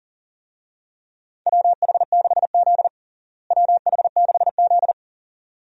Morse code, heard as a keyed tone, about 40 wpm, 700 Hz. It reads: WH6Z WH6Z
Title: USA Call Signs - 4 Characters - Repeated 2x - 40wpm